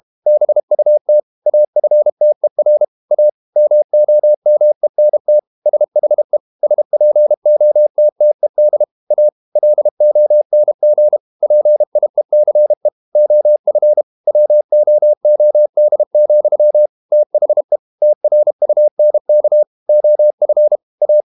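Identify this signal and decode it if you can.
Morse code, heard as a keyed tone, about 32 wpm, 600 Hz. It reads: BUT AFTER A MOMENT SHE SPOTTED A LONG PIECE OF WOOD, THE TRUNK OF A
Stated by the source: Morse code practice at 32 wpm from CW med Gnister, reading Ken Follett's novel Never in English.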